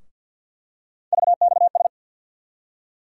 Morse code, heard as a keyed tone, about 47 wpm, 700 Hz. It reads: VXS
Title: USA Call Sign Postfixes - 3 Letters 50wpm